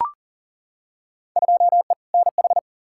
Morse code, heard as a keyed tone, about 40 wpm, 700 Hz. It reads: 2E NH
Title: Field Day Exchanges - 40wpm